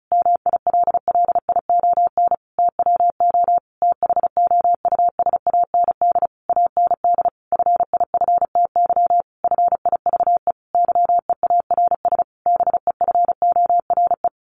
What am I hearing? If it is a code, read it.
Morse code, heard as a keyed tone, about 34 wpm, 700 Hz. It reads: MILLION TWO THOUSAND AND FIFTY FIVE YEARS BEFORE
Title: A Sound Of Thunder 35wpm